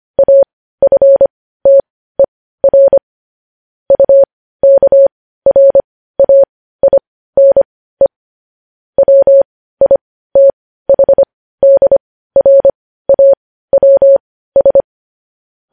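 Morse code, heard as a keyed tone, about 25 wpm, spaced as for 15 wpm, 550 Hz. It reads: AFTER UKRAINE WITHDRAWS
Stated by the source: Morse code news headlines sent at 15 wpm.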